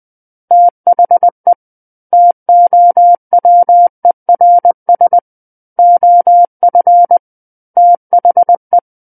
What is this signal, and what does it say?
Morse code, heard as a keyed tone, about 20 wpm, 700 Hz. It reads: THE TOWERS OF THE
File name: 20 WPM